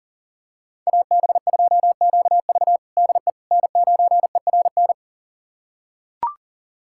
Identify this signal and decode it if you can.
Morse code, heard as a keyed tone, about 37 wpm, 690 Hz. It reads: AB2QV DE N9ERN
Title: USA Callsign DE Callsign - 5 Characters - 40wpm